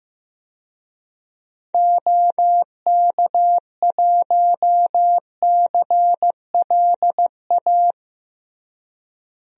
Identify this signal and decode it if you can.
Morse code, heard as a keyed tone, about 16 wpm, 700 Hz. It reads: OK1CLA
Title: DXCC-100 Call Signs - 6 Characters - Repeated 3x - 15wpm